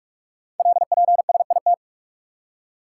Morse code, heard as a keyed tone, about 44 wpm, 700 Hz. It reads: LPSIT
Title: Code Groups - 5 Letters - 45wpm